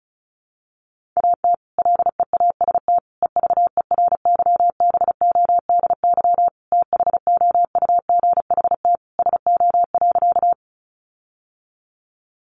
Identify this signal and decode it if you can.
Morse code, heard as a keyed tone, about 34 wpm, 700 Hz. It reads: AT LEAST EVERYBODY THOUGHT SO.
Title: Sentences from Top 1400 Words 35wpm